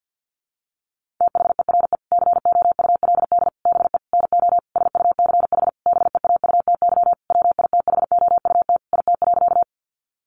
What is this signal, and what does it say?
Morse code, heard as a keyed tone, about 50 wpm, 700 Hz. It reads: THERE COULD BE NO SUCH BEAUTY WITHOUT IT.
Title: Sentences from Top 600 Words 50wpm